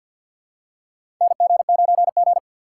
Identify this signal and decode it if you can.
Morse code, heard as a keyed tone, about 52 wpm, 700 Hz. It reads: NG9G
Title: USA Call Signs - 4 Characters - 2x Character Spacing - 50wpm